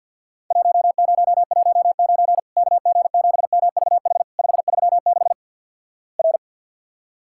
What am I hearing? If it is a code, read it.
Morse code, heard as a keyed tone, about 48 wpm, 700 Hz. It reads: 1019 KG7MVH 536 R
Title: Worked All Europe DX Contest - QTC Exchange with Repeats - 50wpm